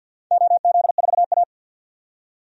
Morse code, heard as a keyed tone, about 51 wpm, 700 Hz. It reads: OZ4A
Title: DXCC-100 Call Signs - 4 Characters - 50wpm